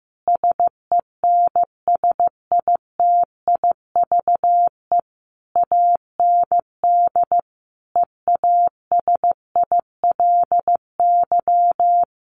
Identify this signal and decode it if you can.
Morse code, heard as a keyed tone, about 15 wpm, 700 Hz. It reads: SENSITIVE AND EASILY
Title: Sentences from Top 1400 Words 15wpm